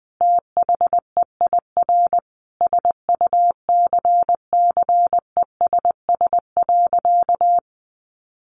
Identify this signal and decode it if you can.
Morse code, heard as a keyed tone, about 20 wpm, 700 Hz. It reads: THEIR SUCCESS.
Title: Sentences from Top 900 Words 20wpm